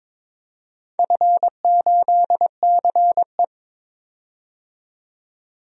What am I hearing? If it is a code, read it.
Morse code, heard as a keyed tone, about 22 wpm, 700 Hz. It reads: F8CE
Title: DXCC-100 Call Signs - 4 Characters - 22wpm